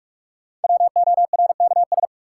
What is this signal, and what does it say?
Morse code, heard as a keyed tone, about 46 wpm, 700 Hz. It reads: WORKS
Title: Sentences from Top 1000 Words - Review Part 5 - New Word Drill - 45wpm